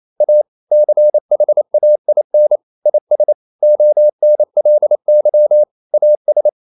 Morse code, heard as a keyed tone, about 28 wpm, 600 Hz. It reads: A CHAIN IS ONLY AS
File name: CW / Practice / EnglishSayings28